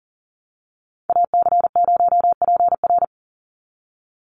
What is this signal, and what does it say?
Morse code, heard as a keyed tone, about 40 wpm, 700 Hz. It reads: AC0PR
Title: NA SOTA Chaser Callsigns 40wpm